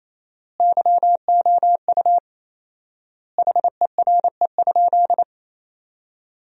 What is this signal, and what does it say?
Morse code, heard as a keyed tone, about 28 wpm, 700 Hz. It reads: YOU HERE?
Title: 4x Word-Spacing - Sentences from Top 500 Words - 28wpm